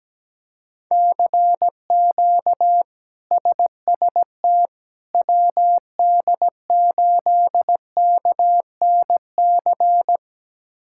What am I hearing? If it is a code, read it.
Morse code, heard as a keyed tone, about 17 wpm, 700 Hz. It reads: CQ SST WD8KNC
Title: K1USN SST QSO Practice - 17wpm